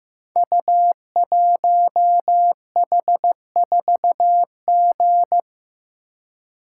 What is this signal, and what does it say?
Morse code, heard as a keyed tone, about 15 wpm, 700 Hz. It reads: U1H4G
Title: Code Groups - 5 Letter:Numbers - 15wpm